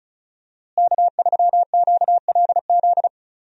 Morse code, heard as a keyed tone, about 36 wpm, 700 Hz. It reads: K3QLZ
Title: USA Call Signs - 5 Characters - Repeated 2x - 35wpm